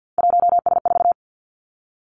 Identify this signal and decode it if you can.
Morse code, heard as a keyed tone, about 50 wpm, 700 Hz. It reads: 1S4
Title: Three Challenging Letter-Number Combinations 50wpm